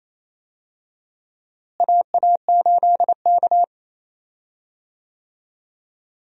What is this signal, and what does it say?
Morse code, heard as a keyed tone, about 28 wpm, 700 Hz. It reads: AA8K
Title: USA Call Signs - 4 Characters - Repeated 3x - 28wpm